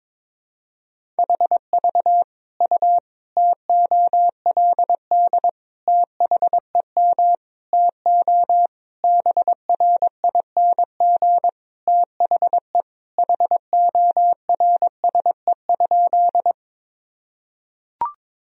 Morse code, heard as a keyed tone, about 22 wpm, 700 Hz. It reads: HV U TOLD THEM TO BRING THE HORSE?
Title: Sentences from Top 500 Words and Abbreviations 22wpm